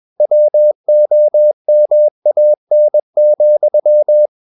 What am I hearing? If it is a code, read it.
Morse code, heard as a keyed tone, about 21 wpm, 600 Hz. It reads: WOMAN,
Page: CW med Gnister